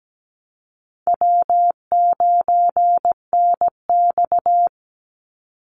Morse code, heard as a keyed tone, about 17 wpm, 700 Hz. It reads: W9NX
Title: Easy Call Signs - 17wpm